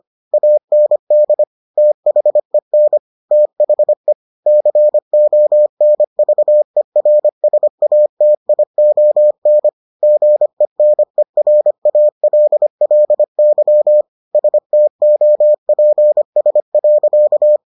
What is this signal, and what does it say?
Morse code, heard as a keyed tone, about 25 wpm, 600 Hz. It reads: AND THEN THE CONVERSATION GENERALLY STOPS.